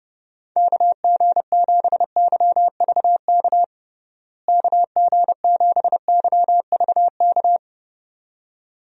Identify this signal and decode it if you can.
Morse code, heard as a keyed tone, about 30 wpm, 700 Hz. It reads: KG7YVK KG7YVK
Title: USA Call Signs - 6 Characters - Repeated 2x - 30wpm